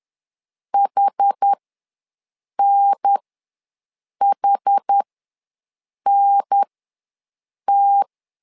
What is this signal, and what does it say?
Morse code, heard as a keyed tone, about 10 wpm, spaced as for 6 wpm, 800 Hz. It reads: HNHNT
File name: 2 UHDN ws 10wpm